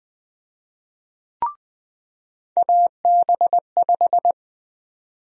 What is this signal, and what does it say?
Morse code, heard as a keyed tone, about 20 wpm, 700 Hz. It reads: AB5
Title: USA Call Sign Prefixes - 20wpm